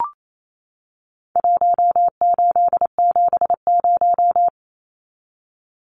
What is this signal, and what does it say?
Morse code, heard as a keyed tone, about 28 wpm, 700 Hz. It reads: 1870
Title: Numbers - Contest Style - 4-Digit 28wpm